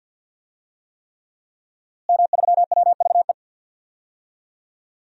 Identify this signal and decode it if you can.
Morse code, heard as a keyed tone, about 46 wpm, 700 Hz. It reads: M3WVE
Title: DXCC-100 Call Signs - 5 Characters - Repeated 3x - 50wpm